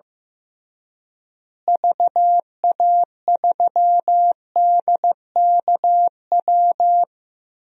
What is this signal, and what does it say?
Morse code, heard as a keyed tone, about 15 wpm, 700 Hz. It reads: VA3DKW
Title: DXCC-100 Call Signs - 6 Characters - Repeated 3x - 15wpm